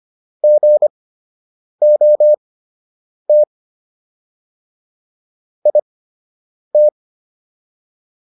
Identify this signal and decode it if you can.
Morse code, heard as a keyed tone, about 25 wpm, spaced as for 8 wpm, 600 Hz. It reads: GOT IT